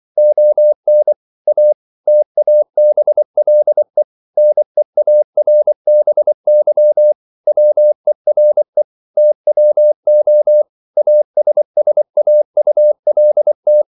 Morse code, heard as a keyed tone, about 24 wpm, 600 Hz. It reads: ON A TABLE NEARBY WERE TWO ASSAULT